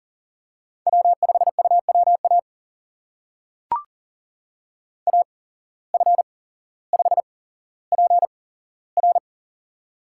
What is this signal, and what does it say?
Morse code, heard as a keyed tone, about 40 wpm, spaced as for 17 wpm, 700 Hz. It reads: W5UWA AF5PR
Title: USA Call Signs - 5 Characters - 8x Character Spacing - 40wpm